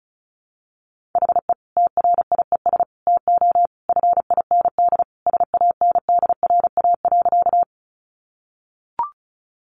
Morse code, heard as a keyed tone, about 34 wpm, 700 Hz. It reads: HE TRIES TO FIND SANDRA.
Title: Easy Phrases 35wpm